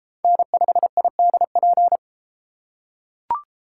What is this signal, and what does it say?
Morse code, heard as a keyed tone, about 30 wpm, 700 Hz. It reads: N5IDP